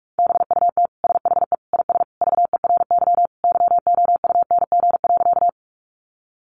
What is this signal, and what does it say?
Morse code, heard as a keyed tone, about 44 wpm, 700 Hz. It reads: BUT SHE IS VERY YOUNG.
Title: Sentences from Top 400 Words 45wpm